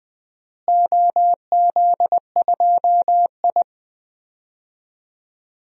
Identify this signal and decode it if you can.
Morse code, heard as a keyed tone, about 21 wpm, 700 Hz. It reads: OZ2I
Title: DXCC-100 Call Signs - 4 Characters - Repeated 2x - 20wpm